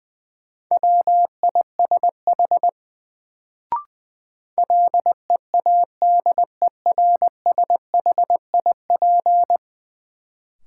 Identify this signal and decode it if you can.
Morse code, heard as a keyed tone, about 20 wpm, 700 Hz. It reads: WISH LEADERSHIP
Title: Sentences from Top 1600 Words - New Word Drill Alt - 20wpm